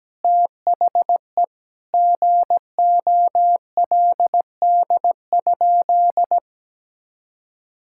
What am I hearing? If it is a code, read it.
Morse code, heard as a keyed tone, about 17 wpm, 700 Hz. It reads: THE GOLD?
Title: Sentences from Top 1000 Words - Review Part 6 - 17wpm